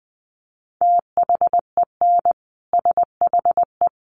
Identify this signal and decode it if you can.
Morse code, heard as a keyed tone, about 19 wpm, 700 Hz. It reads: THEN SHE
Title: Sentences from Top 800 Words 20wpm